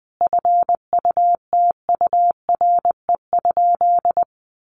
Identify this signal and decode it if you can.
Morse code, heard as a keyed tone, about 20 wpm, 700 Hz. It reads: FUTURE?
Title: Sentences from Top 700 Words 20wpm